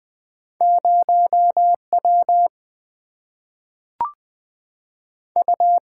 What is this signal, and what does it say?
Morse code, heard as a keyed tone, about 20 wpm, 700 Hz. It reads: 0 W U
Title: Single-Letter Single-Number 20wpm